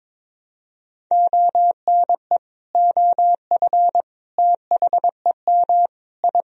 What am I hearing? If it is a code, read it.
Morse code, heard as a keyed tone, about 22 wpm, 700 Hz. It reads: ONE OF THEM I
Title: Sentences from Top 1000 Words - Review Part 1 - 22wpm